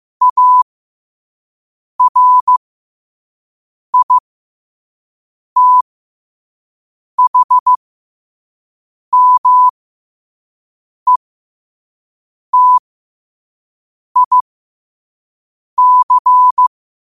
Morse code, heard as a keyed tone, about 15 wpm, spaced as for 5 wpm, 1000 Hz. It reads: ARITHMETIC
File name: qotdfiles / 5wpm